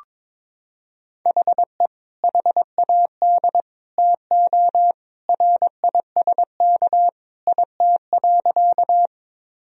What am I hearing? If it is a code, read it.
Morse code, heard as a keyed tone, about 22 wpm, 700 Hz. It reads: HE HAD TO RISK IT.
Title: Sentences from Top 1000 Words - Review Part 4 - 22wpm